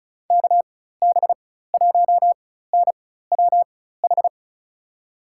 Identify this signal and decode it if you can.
Morse code, heard as a keyed tone, about 36 wpm, spaced as for 16 wpm, 700 Hz. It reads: KB1NWH
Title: USA Call Signs - 4x Character Spacing - 35wpm